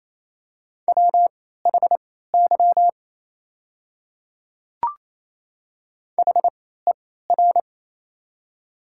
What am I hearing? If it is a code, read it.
Morse code, heard as a keyed tone, about 28 wpm, spaced as for 16 wpm, 700 Hz. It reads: WHY HER
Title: 3x Character Spacing - Top 200 Words - 28wpm